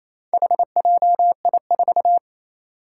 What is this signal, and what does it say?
Morse code, heard as a keyed tone, about 28 wpm, 700 Hz. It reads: HJI4
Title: Four Challenging Letter-Number Combinations 28wpm